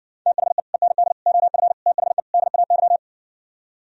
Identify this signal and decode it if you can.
Morse code, heard as a keyed tone, about 60 wpm, 700 Hz. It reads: THE END OF THE DAY